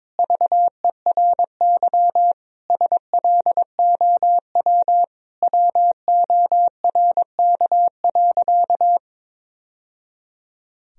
Morse code, heard as a keyed tone, about 21 wpm, 700 Hz. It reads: VERY SLOW WORK.
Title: Sentences from Top 1000 Words - Review Part 2 - 22wpm